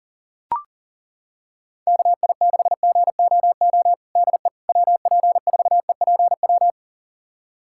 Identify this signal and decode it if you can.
Morse code, heard as a keyed tone, about 37 wpm, 700 Hz. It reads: KI6GOO DE WP4EPW
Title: USA Callsign DE Callsign - 6 Characters - 40wpm